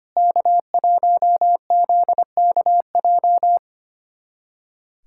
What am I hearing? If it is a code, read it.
Morse code, heard as a keyed tone, about 25 wpm, 700 Hz. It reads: K1ZKJ
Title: USA Callsign DE Callsign 25wpm